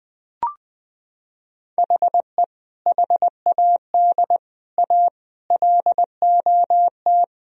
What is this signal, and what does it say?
Morse code, heard as a keyed tone, about 20 wpm, 700 Hz. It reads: HE HAD A LOT
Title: Sentences from Top 1000 Words - Review Part 3 - 20wpm